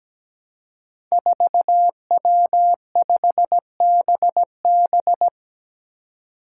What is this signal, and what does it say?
Morse code, heard as a keyed tone, about 17 wpm, 700 Hz. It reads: 4W5BB
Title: Five Challenging Letter-Number Combinations 17wpm